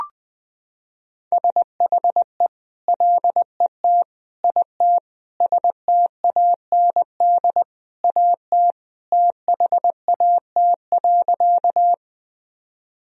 Morse code, heard as a keyed tone, about 20 wpm, 700 Hz. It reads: SHE LET IT STAND AT THAT.
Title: Sentences from Top 1000 Words - Review Part 1 - 20wpm